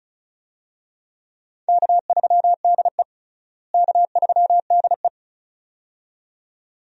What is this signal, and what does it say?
Morse code, heard as a keyed tone, about 36 wpm, 700 Hz. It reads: K3DE K3DE
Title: USA Call Signs - 4 Characters - Repeated 2x - 35wpm